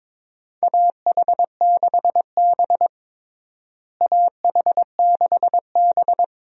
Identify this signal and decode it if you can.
Morse code, heard as a keyed tone, about 22 wpm, 700 Hz. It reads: AH6B AH6B
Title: USA Call Signs - 4 Characters - Repeated 2x - 22wpm